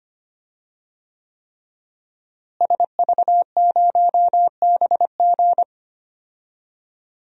Morse code, heard as a keyed tone, about 25 wpm, 700 Hz. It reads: SV0BG
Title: DXCC-100 Call Signs - 5 Characters - Repeated 3x - 25wpm